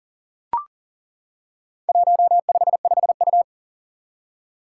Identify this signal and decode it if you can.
Morse code, heard as a keyed tone, about 41 wpm, 700 Hz. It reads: 155U
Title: Four Challenging Letter-Number Combinations 40wpm